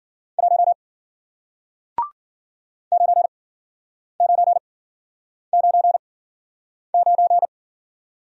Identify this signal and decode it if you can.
Morse code, heard as a keyed tone, about 52 wpm, spaced as for 9 wpm, 700 Hz. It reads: 1 9999